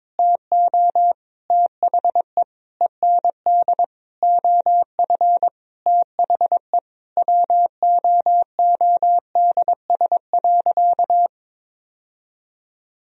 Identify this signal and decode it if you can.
Morse code, heard as a keyed tone, about 22 wpm, 700 Hz. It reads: TO THE END OF THE WOODS.
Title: Sentences from Top 1000 Words - Review Part 2 - 22wpm